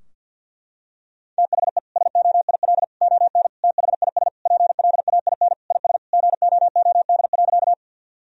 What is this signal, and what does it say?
Morse code, heard as a keyed tone, about 50 wpm, 700 Hz. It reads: THE SOIL ON THIS PLAIN IS GOOD.
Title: Sentences from Top 1000 Words - Review Part 4 - 50wpm